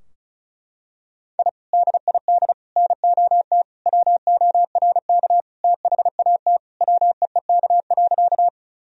Morse code, heard as a keyed tone, about 36 wpm, 700 Hz. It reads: I DID NOT WORK THAT WEEK.
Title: Sentences from Top 1000 Words - Review Part 2 - 35wpm